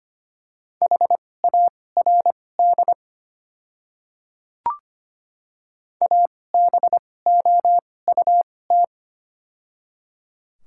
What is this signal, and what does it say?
Morse code, heard as a keyed tone, about 25 wpm, spaced as for 18 wpm, 700 Hz. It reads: HARD ABOUT
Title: 2x Character Spacing - Top 300 Words - 25wpm